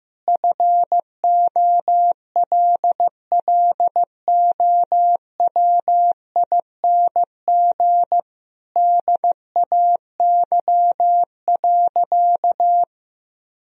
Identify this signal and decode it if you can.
Morse code, heard as a keyed tone, about 15 wpm, 700 Hz. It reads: FOLLOWING DAY.
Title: Sentences from Top 1000 Words - Review Part 4 - 15wpm